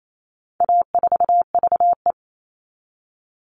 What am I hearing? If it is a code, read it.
Morse code, heard as a keyed tone, about 28 wpm, 700 Hz. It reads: A4VE
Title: All Cut Numbers - 4-Digit 28wpm